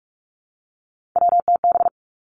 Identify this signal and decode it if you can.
Morse code, heard as a keyed tone, about 46 wpm, 700 Hz. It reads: WTB
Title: Three Challenging Letter-Number Combinations 45wpm